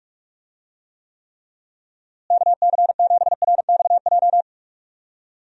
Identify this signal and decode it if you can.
Morse code, heard as a keyed tone, about 41 wpm, 700 Hz. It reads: KC7RXJ
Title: USA Call Signs - 6 Characters - Repeated 3x - 45wpm